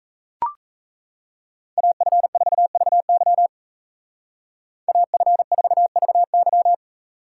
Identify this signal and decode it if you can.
Morse code, heard as a keyed tone, about 33 wpm, 700 Hz. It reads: AF4VY AF4VY
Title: Speed Racing - USA 2x2 Call Signs 28wpm